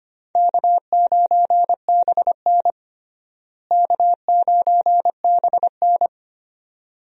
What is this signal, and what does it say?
Morse code, heard as a keyed tone, about 25 wpm, 700 Hz. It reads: K9BN K9BN
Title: USA Call Signs - 4 Characters - Repeated 2x - 25wpm